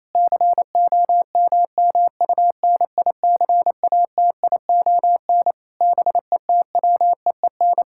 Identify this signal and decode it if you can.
Morse code, heard as a keyed tone, about 28 wpm, 700 Hz. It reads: COMMUNICATION BETWEEN
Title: Sentences from Top 1000 Words - Review Part 6 - 28wpm